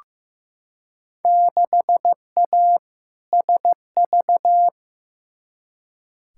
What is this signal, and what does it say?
Morse code, heard as a keyed tone, about 15 wpm, 700 Hz. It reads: 6A SV
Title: Field Day Exchanges Encore - 15wpm